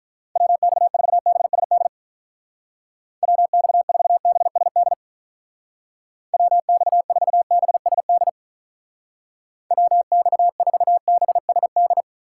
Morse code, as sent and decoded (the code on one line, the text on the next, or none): .-- -..- ....- -... ... -.. / .-- -..- ....- -... ... -.. / .-- -..- ....- -... ... -.. / .-- -..- ....- -... ... -..
WX4BSD WX4BSD WX4BSD WX4BSD